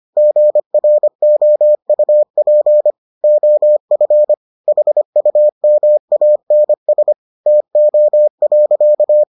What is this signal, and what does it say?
Morse code, heard as a keyed tone, about 25 wpm, 600 Hz. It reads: GROUP OF HUMANS TO.